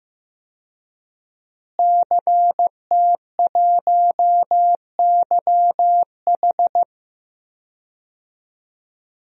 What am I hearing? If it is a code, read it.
Morse code, heard as a keyed tone, about 15 wpm, 700 Hz. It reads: CT1YH
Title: DXCC-100 Call Signs - 5 Characters - Repeated 2x - 15wpm